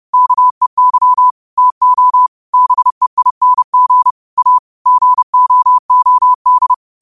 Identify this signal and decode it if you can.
Morse code, heard as a keyed tone, about 30 wpm, 1000 Hz. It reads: KEY TO BEING A GOOD
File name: qotdfiles / 30wpm